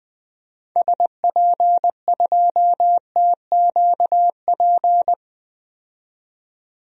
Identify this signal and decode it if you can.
Morse code, heard as a keyed tone, about 20 wpm, 700 Hz. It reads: SP2TQP
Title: DXCC-100 Call Signs - 6 Characters - 20wpm